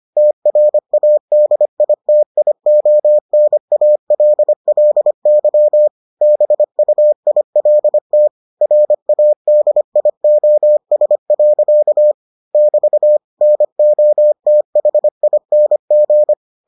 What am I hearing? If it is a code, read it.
Morse code, heard as a keyed tone, about 25 wpm, 600 Hz. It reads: TRADITIONALLY BUILT RADIOS. <BT> NOTHING